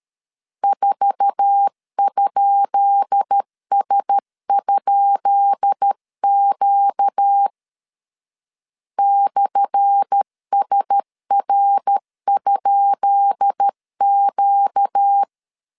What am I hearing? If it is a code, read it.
Morse code, heard as a keyed tone, about 13 wpm, 800 Hz. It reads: 4?S?Q /SR?Q
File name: LEZIONI / 10 SIGNS / 10B INT 12wpm 2414